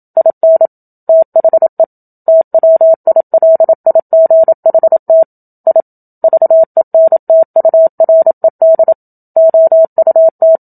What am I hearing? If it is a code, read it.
Morse code, heard as a keyed tone, about 27 wpm, 650 Hz. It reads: IN THE TWILIGHT I VENTURED OUT
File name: Book Two / 01-Chapter 1 - 27 WPM